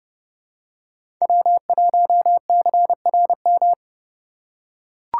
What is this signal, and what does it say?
Morse code, heard as a keyed tone, about 30 wpm, 700 Hz. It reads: W1CRM
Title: USA Call Signs - 2x Character Spacing - 30wpm